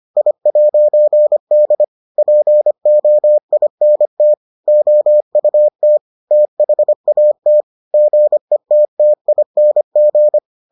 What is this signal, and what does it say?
Morse code, heard as a keyed tone, about 25 wpm, 600 Hz. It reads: I'D POINT OUT THAT GETTING